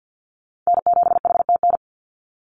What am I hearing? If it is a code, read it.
Morse code, heard as a keyed tone, about 50 wpm, 700 Hz. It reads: N7HTN
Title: USA Call Signs 50wpm